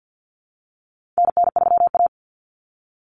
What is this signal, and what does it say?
Morse code, heard as a keyed tone, about 50 wpm, 700 Hz. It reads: NN3A
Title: Common Cut Numbers - 4-Digit 50wpm